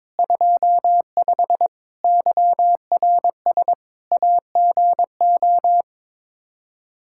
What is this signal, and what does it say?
Morse code, heard as a keyed tone, about 22 wpm, 700 Hz. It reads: 25 YRS AGO
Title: Practice QSO - Exchange 3 - More Info - 22wpm